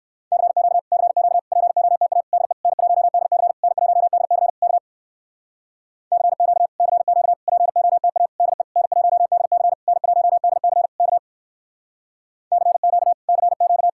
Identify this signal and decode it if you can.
Morse code, heard as a keyed tone, about 59 wpm, 700 Hz. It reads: CQ CQ POTA DE N1DC N1DC K CQ CQ POTA DE N1DC N1DC K CQ CQ